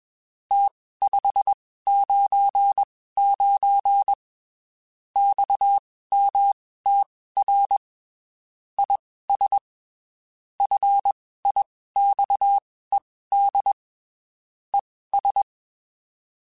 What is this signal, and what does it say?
Morse code, heard as a keyed tone, about 21 wpm, spaced as for 15 wpm, 800 Hz. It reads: T599 XMTR IS FIXED ES
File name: (spaced) / QSOs Spaced-mp3s / WA7QOX 1 @20 sp